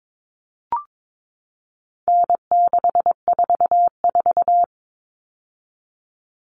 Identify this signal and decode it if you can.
Morse code, heard as a keyed tone, about 22 wpm, 700 Hz. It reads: N644
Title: All Cut Numbers - 4-Digit 22wpm